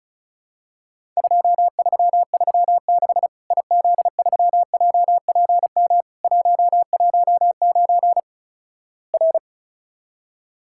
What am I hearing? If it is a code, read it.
Morse code, heard as a keyed tone, about 36 wpm, 700 Hz. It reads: 2336 IZ3JPM 119 R
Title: Worked All Europe DX Contest - QTC Exchange - 35wpm